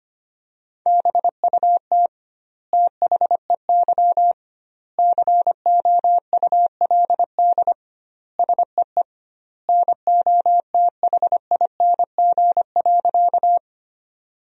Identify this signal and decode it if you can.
Morse code, heard as a keyed tone, about 25 wpm, 700 Hz. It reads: BUT THEY COULD SEE NOTHING.
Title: 2x Word-Spacing - Sentences from Top 500 Words - 25wpm